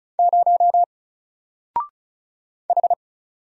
Morse code, heard as a keyed tone, about 34 wpm, 700 Hz. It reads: 0 H
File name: Single-Letter-Number-35wpm